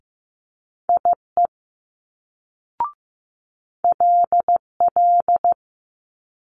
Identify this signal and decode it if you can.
Morse code, heard as a keyed tone, about 15 wpm, 700 Hz. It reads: IE LL
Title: Multi-letter Phonemes Quick Fire 15wpm